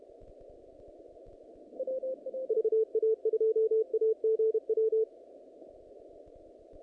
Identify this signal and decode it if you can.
Morse code, heard as a keyed tone, about 32 wpm, 440 Hz. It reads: VA2AGW
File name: contests / naqp-cw-k5tr-op-k5pi- -2200